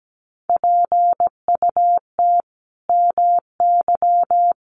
Morse code, heard as a keyed tone, about 17 wpm, 700 Hz. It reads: PUT MY
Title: Sentences from Top 1400 Words 17wpm